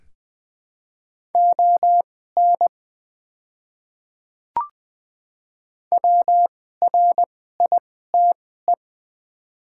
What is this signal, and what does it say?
Morse code, heard as a keyed tone, about 20 wpm, spaced as for 15 wpm, 690 Hz. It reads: ON WRITE
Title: 2x Character Spacing - Top 100 Words - 20wpm